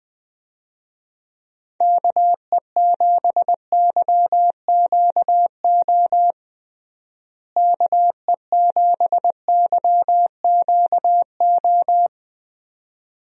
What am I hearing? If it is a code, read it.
Morse code, heard as a keyed tone, about 20 wpm, 700 Hz. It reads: KE7YQO KE7YQO
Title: USA Call Signs - 6 Characters - Repeated 2x - 20wpm